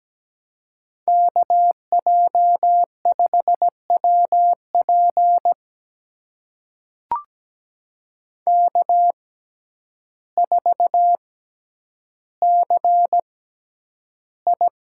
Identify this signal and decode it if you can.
Morse code, heard as a keyed tone, about 17 wpm, spaced as for 9 wpm, 700 Hz. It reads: KJ5WP K4CI
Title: USA Call Signs - 6x Character Spacing - 17wpm